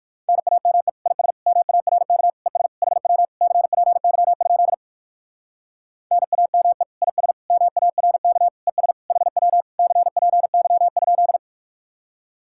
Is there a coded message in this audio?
Yes: Morse code from 0.3 to 11.4 s, heard as a keyed tone, about 58 wpm, 700 Hz. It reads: NAME IS MARK ES HW CPY? NAME IS MARK ES HW CPY?